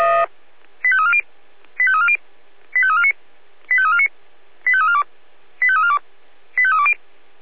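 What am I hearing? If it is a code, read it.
Five-tone selective call (ZVEI), heard as quick runs of five tones, tones of about 70 ms. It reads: 85340 85340 85349 85349 85342 85342 85320
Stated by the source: Five-tone ZVEI selective call received by SDR.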